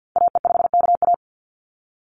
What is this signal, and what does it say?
Morse code, heard as a keyed tone, about 50 wpm, 700 Hz. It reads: AE5KA
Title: NA SOTA Activator Callsigns 50wpm